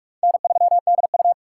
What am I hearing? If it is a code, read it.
Morse code, heard as a keyed tone, about 46 wpm, 700 Hz. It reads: N3DU
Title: USA Call Signs - 4 Characters - 6x Character Spacing - 45wpm